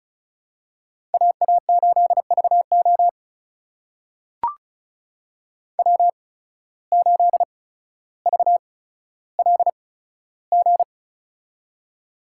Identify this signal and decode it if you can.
Morse code, heard as a keyed tone, about 36 wpm, spaced as for 15 wpm, 700 Hz. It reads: AA8VO W8VLG
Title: USA Call Signs - 5 Characters - 8x Character Spacing - 35wpm